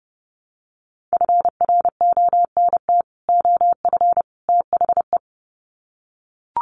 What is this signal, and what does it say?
Morse code, heard as a keyed tone, about 30 wpm, 700 Hz. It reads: FRONT OF THE
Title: Sets of 3 Words 30wpm